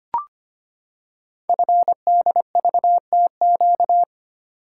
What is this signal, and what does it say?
Morse code, heard as a keyed tone, about 25 wpm, 700 Hz. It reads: FDVTQ
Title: Code Groups - 5 Letters - 25wpm